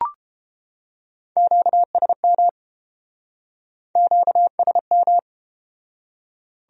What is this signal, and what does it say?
Morse code, heard as a keyed tone, about 32 wpm, 700 Hz. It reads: QSM QSM